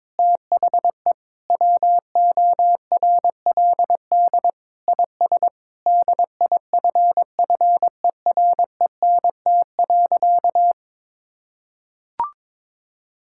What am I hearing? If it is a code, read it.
Morse code, heard as a keyed tone, about 22 wpm, 700 Hz. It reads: THE WORLD IS DIFFERENT.
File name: practice / 22wpm / Sentences with 4-Word Phrases - 22wpm